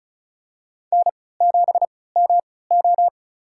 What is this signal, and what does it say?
Morse code, heard as a keyed tone, about 36 wpm, spaced as for 20 wpm, 700 Hz. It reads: N7MO
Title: USA Call Signs - 4 Characters - 3x Character Spacing - 35wpm